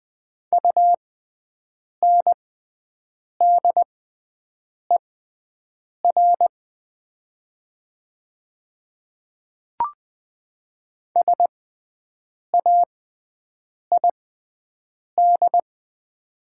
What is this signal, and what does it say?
Morse code, heard as a keyed tone, about 20 wpm, spaced as for 7 wpm, 700 Hz. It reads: UNDER SAID
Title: 6x Character Spacing - Top 500 Words - 20wpm